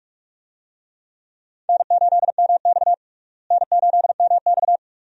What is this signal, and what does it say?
Morse code, heard as a keyed tone, about 46 wpm, 700 Hz. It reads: N8MX N8MX
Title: USA Call Signs - 4 Characters - Repeated 2x - 45wpm